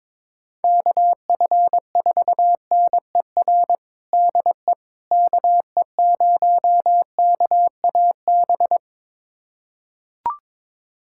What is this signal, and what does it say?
Morse code, heard as a keyed tone, about 22 wpm, 700 Hz. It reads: KF4NER DE KE0KAB
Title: USA Callsign DE Callsign 22wpm